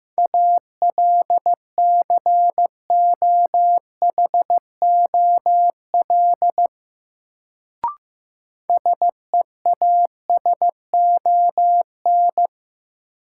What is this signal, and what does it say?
Morse code, heard as a keyed tone, about 15 wpm, 700 Hz. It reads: ALCOHOL SEASON
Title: Sentences from Top 1400 Words - New Word Drill - 15wpm